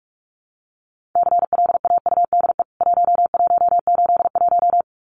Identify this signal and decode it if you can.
Morse code, heard as a keyed tone, about 44 wpm, 700 Hz. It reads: CLAUDE 1181
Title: CWT Exchange 45wpm